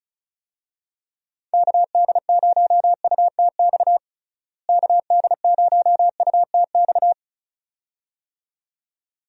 Morse code, heard as a keyed tone, about 34 wpm, 700 Hz. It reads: KD0UTX KD0UTX
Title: USA Call Signs - 6 Characters - Repeated 2x - 35wpm